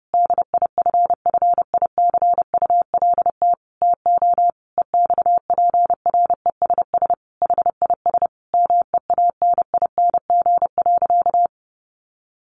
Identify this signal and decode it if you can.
Morse code, heard as a keyed tone, about 30 wpm, 700 Hz. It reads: DIFFICULT TO EXPRESS HIS MEANING.
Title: Sentences from Top 1100 Words 30wpm